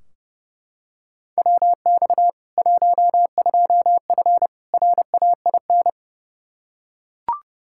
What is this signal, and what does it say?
Morse code, heard as a keyed tone, about 30 wpm, 700 Hz. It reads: WX 12F RAIN
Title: K1USN SST QSO Practice 30wpm